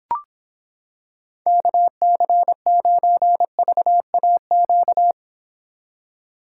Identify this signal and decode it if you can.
Morse code, heard as a keyed tone, about 26 wpm, 700 Hz. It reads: KC9VAQ